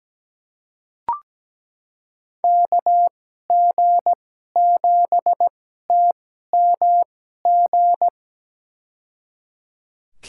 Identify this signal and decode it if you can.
Morse code, heard as a keyed tone, about 17 wpm, spaced as for 12 wpm, 700 Hz. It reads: KG7TMG
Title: USA Call Signs - 6 Characters - 2x Character Spacing - 17wpm